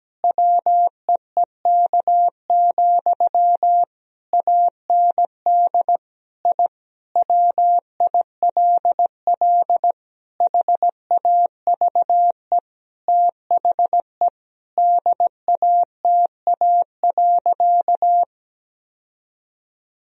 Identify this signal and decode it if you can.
Morse code, heard as a keyed tone, about 17 wpm, 700 Hz. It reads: WEEK, AND I WILL HAVE THE DATA.
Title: Sentences from Top 1000 Words - Review Part 3 - 17wpm